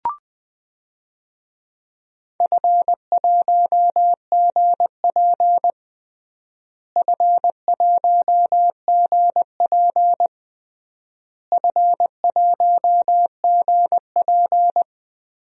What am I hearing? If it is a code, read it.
Morse code, heard as a keyed tone, about 21 wpm, 700 Hz. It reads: F1GP F1GP F1GP
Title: DXCC-100 Call Signs - 4 Characters - Repeated 3x - 20wpm